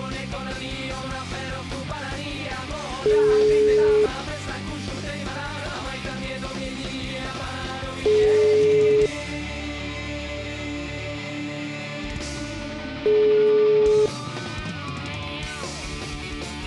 Ringback tone (Europe, 425 Hz), with noise.